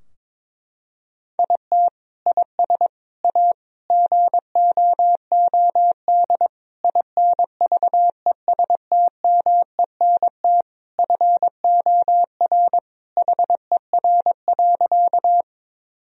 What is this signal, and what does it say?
Morse code, heard as a keyed tone, about 22 wpm, 700 Hz. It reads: IT IS A GOOD INVESTMENT FOR HER.
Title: Sentences from Top 1000 Words - Review Part 4 - 22wpm